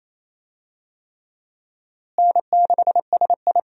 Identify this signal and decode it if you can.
Morse code, heard as a keyed tone, about 27 wpm, 700 Hz. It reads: N6SI
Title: USA Call Signs - 4 Characters - Repeated 3x - 28wpm